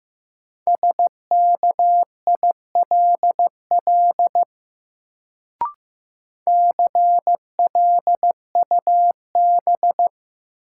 Morse code, heard as a keyed tone, about 15 wpm, 700 Hz. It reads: SKILL CLUB
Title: Sentences from Top 1200 Words - New Word Drill - 15wpm